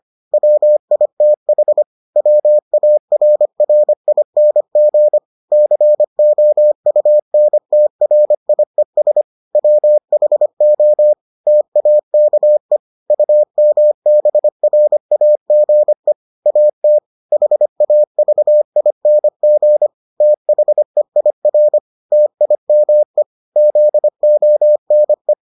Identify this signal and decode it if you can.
Morse code, heard as a keyed tone, about 25 wpm, 600 Hz. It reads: WITH WARRING COUNTRIES WHO TAKE UMBRAGE AT HAVING THEIR TIME ZONE